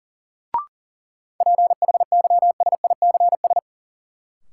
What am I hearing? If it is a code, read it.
Morse code, heard as a keyed tone, about 40 wpm, 700 Hz. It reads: PHYSICS
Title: Sentences from Top 1500 Words - New Word Drill Alt - 40wpm